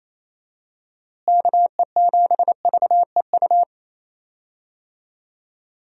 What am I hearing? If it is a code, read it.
Morse code, heard as a keyed tone, about 28 wpm, 700 Hz. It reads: KE7VEU
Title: USA Call Signs - 6 Characters - Repeated 2x - 28wpm